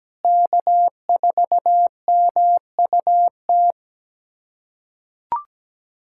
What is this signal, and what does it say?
Morse code, heard as a keyed tone, about 17 wpm, 700 Hz. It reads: K4MUT
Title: USA Callsign DE Callsign - 5 Characters - 17wpm